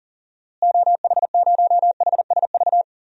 Morse code, heard as a keyed tone, about 40 wpm, 700 Hz. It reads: OH0HSV
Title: DXCC-100 Call Signs - 6 Characters - Repeated 2x - 40wpm